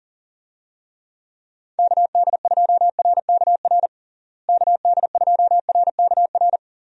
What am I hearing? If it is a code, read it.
Morse code, heard as a keyed tone, about 37 wpm, 700 Hz. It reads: KD2RKR KD2RKR
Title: USA Call Signs - 6 Characters - Repeated 2x - 40wpm